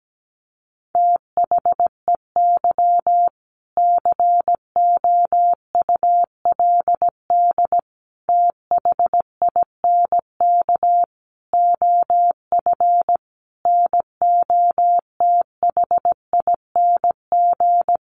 Morse code, heard as a keyed tone, about 17 wpm, 700 Hz. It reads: THEY COULD THINK OF NOTHING
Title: Sentences from Top 800 Words 17wpm